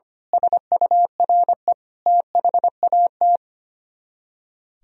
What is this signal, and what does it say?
Morse code, heard as a keyed tone, about 24 wpm, 700 Hz. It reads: SURE THAT
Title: Speed-Racing - Sets of 5 Words 25wpm